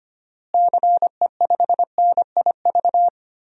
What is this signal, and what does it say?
Morse code, heard as a keyed tone, about 25 wpm, 700 Hz. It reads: CE5NIV
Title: DXCC-100 Call Signs - 6 Characters - Repeated 3x - 25wpm